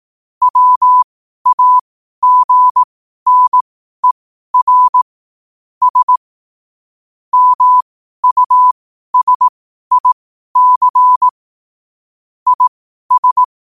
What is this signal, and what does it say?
Morse code, heard as a keyed tone, about 18 wpm, spaced as for 12 wpm, 1000 Hz. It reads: WAGNER S MUSIC IS
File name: qotdfiles / 12wpm